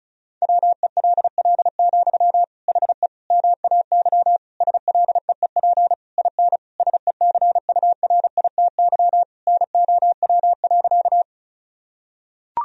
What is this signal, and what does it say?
Morse code, heard as a keyed tone, about 36 wpm, 700 Hz. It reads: WELL, HE MAY SLEEP IN SECURITY NOW.
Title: Sentences from Top 1000 Words - Review Part 4 - 35wpm